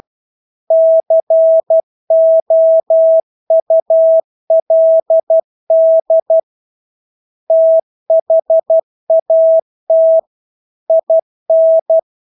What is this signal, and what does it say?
Morse code, heard as a keyed tone, about 12 wpm, 650 Hz. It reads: COULD THAT IN
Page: Engelske ord 12 wpm